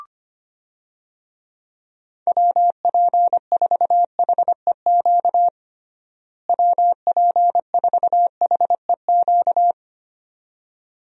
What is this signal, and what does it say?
Morse code, heard as a keyed tone, about 25 wpm, 700 Hz. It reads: WP4HEQ WP4HEQ
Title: USA Call Signs - 6 Characters - Repeated 2x - 25wpm